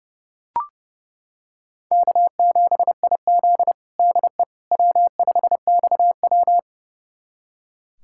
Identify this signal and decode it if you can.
Morse code, heard as a keyed tone, about 28 wpm, 700 Hz. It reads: K7IZ DE W5XW
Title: USA Callsign DE Callsign - 4 Characters - 30wpm